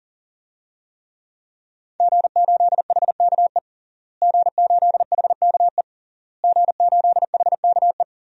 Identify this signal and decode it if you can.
Morse code, heard as a keyed tone, about 37 wpm, 700 Hz. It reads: G8HKE G8HKE G8HKE
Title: DXCC-100 Call Signs - 5 Characters - Repeated 3x - 40wpm